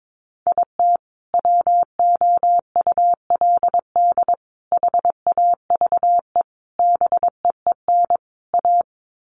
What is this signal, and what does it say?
Morse code, heard as a keyed tone, about 22 wpm, 700 Hz. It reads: IT WOULD HAVE BEEN A
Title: Sets of 5 Words Encore 22wpm